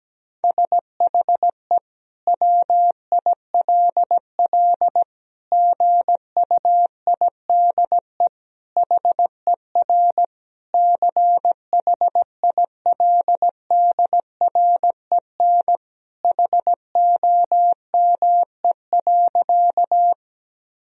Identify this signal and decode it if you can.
Morse code, heard as a keyed tone, about 17 wpm, 700 Hz. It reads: SHE WILL GUIDE HER CHILDREN HOME.
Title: Sentences from Top 1000 Words - Review Part 3 - 17wpm